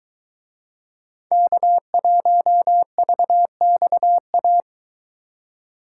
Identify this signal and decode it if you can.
Morse code, heard as a keyed tone, about 22 wpm, 700 Hz. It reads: K1VXA